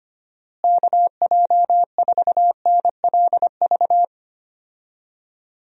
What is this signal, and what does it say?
Morse code, heard as a keyed tone, about 25 wpm, 700 Hz. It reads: KJ4NLV
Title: USA Call Signs - 6 Characters - Repeated 2x - 25wpm